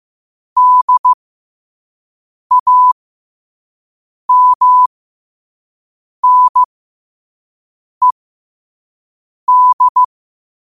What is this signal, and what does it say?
Morse code, heard as a keyed tone, about 15 wpm, spaced as for 5 wpm, 1000 Hz. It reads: DAMNED
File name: qotdfiles / 5wpm